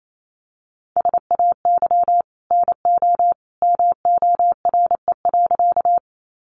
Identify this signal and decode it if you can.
Morse code, heard as a keyed tone, about 28 wpm, 700 Hz. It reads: SAY NO MORE.
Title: Sentences from Top 200 Words 28wpm